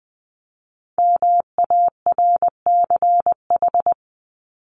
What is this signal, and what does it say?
Morse code, heard as a keyed tone, about 20 wpm, 700 Hz. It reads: MARCH
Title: Top 1501-1750 Words 20wpm